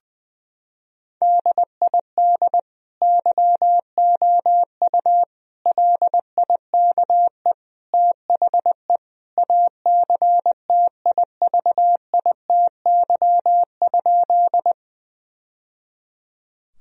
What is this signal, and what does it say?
Morse code, heard as a keyed tone, about 21 wpm, 700 Hz. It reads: DID YOU LIKE THE ACTIVITY?
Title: Sentences from Top 1000 Words - Review Part 4 - 20wpm